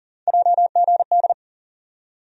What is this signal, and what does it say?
Morse code, heard as a keyed tone, about 39 wpm, 700 Hz. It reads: JGD
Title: USA Call Sign Postfixes - 3 Letters 40wpm